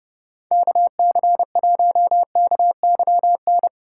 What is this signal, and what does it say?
Morse code, heard as a keyed tone, about 32 wpm, 700 Hz. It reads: KC1KYN